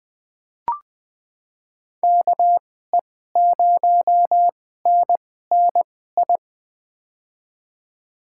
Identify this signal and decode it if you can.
Morse code, heard as a keyed tone, about 20 wpm, spaced as for 15 wpm, 700 Hz. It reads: KE0NNI
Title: USA Call Signs - 6 Characters - 2x Character Spacing - 20wpm